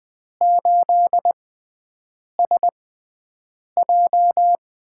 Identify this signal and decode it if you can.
Morse code, heard as a keyed tone, about 20 wpm, spaced as for 7 wpm, 700 Hz. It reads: 8SJ
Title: USA Call Signs - 6x Character Spacing - 20wpm